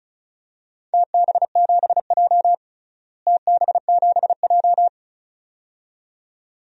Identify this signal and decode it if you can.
Morse code, heard as a keyed tone, about 36 wpm, 700 Hz. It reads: TB7J TB7J
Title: DXCC-100 Call Signs - 4 Characters - Repeated 2x - 35wpm